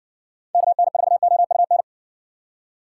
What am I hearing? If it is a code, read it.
Morse code, heard as a keyed tone, about 61 wpm, 700 Hz. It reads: KN4OUN